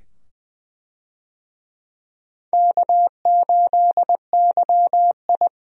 Morse code, heard as a keyed tone, about 20 wpm, 700 Hz. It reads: K8YI
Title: USA Call Signs - 4 Characters - 20wpm